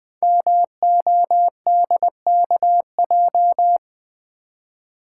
Morse code, heard as a keyed tone, about 20 wpm, 700 Hz. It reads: MODKJ
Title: Code Groups - 5 Letter:Numbers - 20wpm